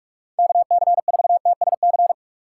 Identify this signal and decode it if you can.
Morse code, heard as a keyed tone, about 46 wpm, 700 Hz. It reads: KC4TSC
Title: USA Call Signs - 6 Characters - 6x Character Spacing - 45wpm